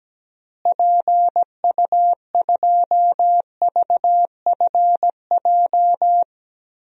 Morse code, heard as a keyed tone, about 17 wpm, 700 Hz. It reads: PU2VFJ
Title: DXCC-100 Call Signs - 6 Characters - Repeated 2x - 17wpm